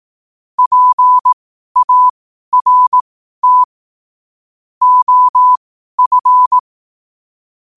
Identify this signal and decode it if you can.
Morse code, heard as a keyed tone, about 18 wpm, spaced as for 12 wpm, 1000 Hz. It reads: PART OF
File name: qotdfiles / 12wpm